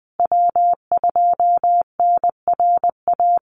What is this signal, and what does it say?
Morse code, heard as a keyed tone, about 20 wpm, 700 Hz. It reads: W2NRA
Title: Call Signs Repeated 3x - 20wpm